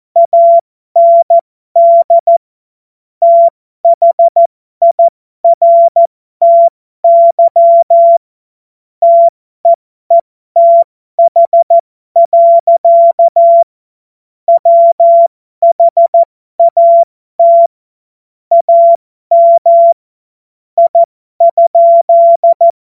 Morse code, heard as a keyed tone, about 14 wpm, spaced as for 12 wpm, 680 Hz. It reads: AND THIRTY TEETH. WHAT AM I?
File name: Morsekurs / Witze